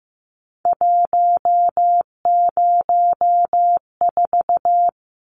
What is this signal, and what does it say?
Morse code, heard as a keyed tone, about 15 wpm, 700 Hz. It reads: 104
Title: Numbers - 3-Digit 15wpm